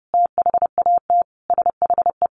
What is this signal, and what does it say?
Morse code, heard as a keyed tone, about 28 wpm, 700 Hz. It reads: THAT SHE
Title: Sentences from Top 800 Words 30wpm